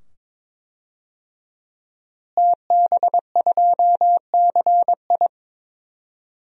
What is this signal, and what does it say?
Morse code, heard as a keyed tone, about 22 wpm, 700 Hz. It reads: TB2CI
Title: DXCC-100 Call Signs - 5 Characters - Repeated 2x - 22wpm